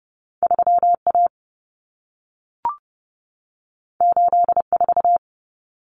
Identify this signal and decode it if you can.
Morse code, heard as a keyed tone, about 30 wpm, 700 Hz. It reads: 3A 84
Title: Common Cut Numbers - 2-Digit 30wpm